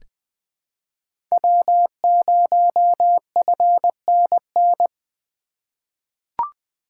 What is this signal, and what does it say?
Morse code, heard as a keyed tone, about 21 wpm, 700 Hz. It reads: W0FNN
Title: USA Call Signs - 5 Characters - 3x Character Spacing - 20wpm